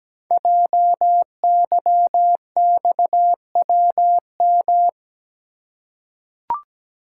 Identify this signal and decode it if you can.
Morse code, heard as a keyed tone, about 17 wpm, 700 Hz. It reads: JYXWM
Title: Code Groups - 5 Letters - 17wpm